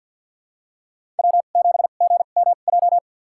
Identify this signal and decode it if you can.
Morse code, heard as a keyed tone, about 47 wpm, spaced as for 35 wpm, 700 Hz. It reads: W7GMJ
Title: USA Call Signs - 5 Characters - 2x Character Spacing - 50wpm